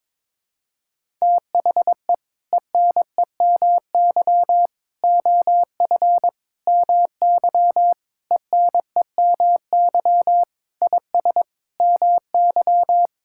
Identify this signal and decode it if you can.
Morse code, heard as a keyed tone, about 21 wpm, 700 Hz. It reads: THE ENEMY OF MY ENEMY IS MY
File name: practice / 22wpm / Sets of 8 Words - 22wpm